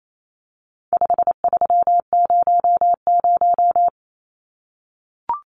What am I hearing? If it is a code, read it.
Morse code, heard as a keyed tone, about 28 wpm, 700 Hz. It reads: 5300
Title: Numbers - 4-Digit 28wpm